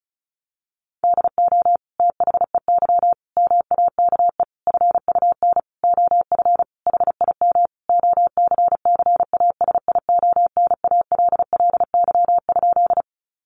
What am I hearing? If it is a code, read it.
Morse code, heard as a keyed tone, about 34 wpm, 700 Hz. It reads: DO THEY MAKE FUN OF HIM OCCASIONALLY?
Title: Sentences from Top 1300 Words 35wpm